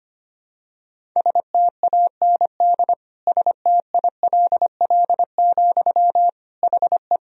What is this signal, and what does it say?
Morse code, heard as a keyed tone, about 25 wpm, 700 Hz. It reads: STAND STILL, HE
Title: Sentences from Top 1000 Words - Review Part 1 - 25wpm